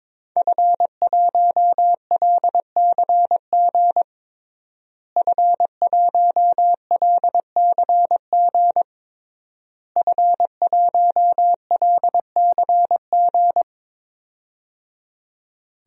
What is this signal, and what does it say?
Morse code, heard as a keyed tone, about 21 wpm, 700 Hz. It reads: F1LCG F1LCG F1LCG
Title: DXCC-100 Call Signs - 5 Characters - Repeated 3x - 22wpm